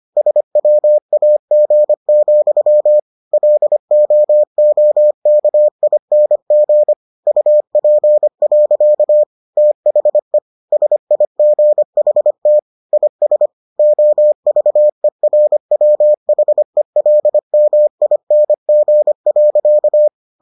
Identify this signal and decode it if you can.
Morse code, heard as a keyed tone, about 25 wpm, 600 Hz. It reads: SWAG, LOOKING UP. THE SIGHT IS OVERWHELMING.